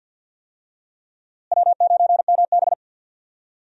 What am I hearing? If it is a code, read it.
Morse code, heard as a keyed tone, about 51 wpm, 700 Hz. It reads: W9MB
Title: USA Call Signs - 4 Characters - Repeated 3x - 50wpm